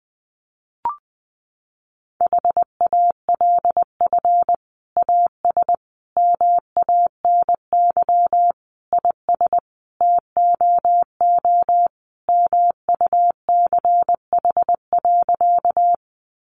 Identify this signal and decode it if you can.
Morse code, heard as a keyed tone, about 20 wpm, 700 Hz. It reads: HALF AS MANY IS TOO MUCH.
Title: Sentences from Top 400 Words 20wpm